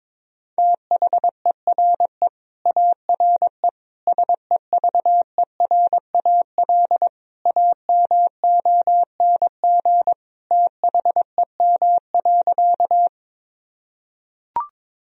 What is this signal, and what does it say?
Morse code, heard as a keyed tone, about 22 wpm, 700 Hz. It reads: THERE ARE SEVERAL AMONG THEM.
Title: Sentences from Top 1000 Words - Review Part 2 - 22wpm